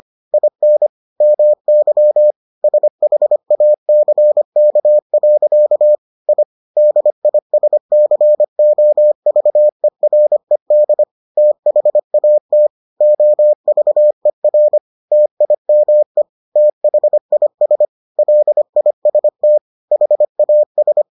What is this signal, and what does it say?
Morse code, heard as a keyed tone, about 25 wpm, 600 Hz. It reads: IN MY SHACK. I DISCOVERED THAT OVER TIME THIS LIST HAS